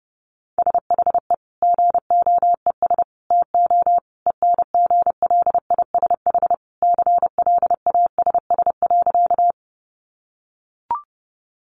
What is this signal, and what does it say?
Morse code, heard as a keyed tone, about 30 wpm, 700 Hz. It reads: SHE GOES TO ENGLISH CLASS.
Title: Additional Easy Phrases 30wpm